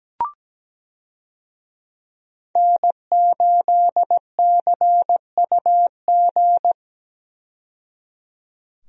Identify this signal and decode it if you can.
Morse code, heard as a keyed tone, about 17 wpm, 700 Hz. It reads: N8CUG
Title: USA Call Signs - 5 Characters - 17wpm